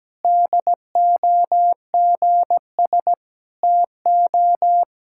Morse code, heard as a keyed tone, about 17 wpm, 700 Hz. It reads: DOGS TO
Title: Pangram Challenge v2 - 17wpm